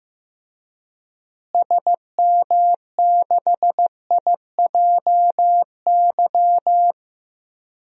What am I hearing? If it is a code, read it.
Morse code, heard as a keyed tone, about 15 wpm, 700 Hz. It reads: SM6IJY
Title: DXCC-100 Call Signs - 6 Characters - Repeated 2x - 15wpm